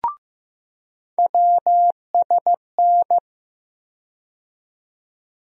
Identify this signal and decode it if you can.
Morse code, heard as a keyed tone, about 15 wpm, 700 Hz. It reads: WSN
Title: USA Call Sign Postfixes - 3 Letters 15wpm